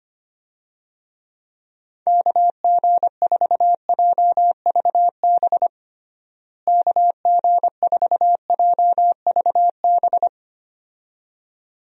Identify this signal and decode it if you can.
Morse code, heard as a keyed tone, about 25 wpm, 700 Hz. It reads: KG4JVB KG4JVB
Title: USA Call Signs - 6 Characters - Repeated 2x - 25wpm